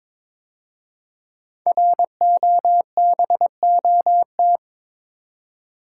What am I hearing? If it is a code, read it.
Morse code, heard as a keyed tone, about 22 wpm, 700 Hz. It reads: ROBOT
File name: 5 Letter Words - First 3 Letters Given - 22wpm